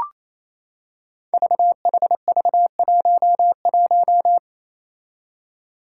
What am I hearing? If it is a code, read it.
Morse code, heard as a keyed tone, about 28 wpm, 700 Hz. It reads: VHV11
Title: Five Challenging Letter-Number Combinations 28wpm